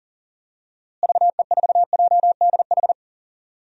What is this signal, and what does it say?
Morse code, heard as a keyed tone, about 38 wpm, 700 Hz. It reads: VE4JDH
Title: DXCC-100 Call Signs - 6 Characters - Repeated 2x - 40wpm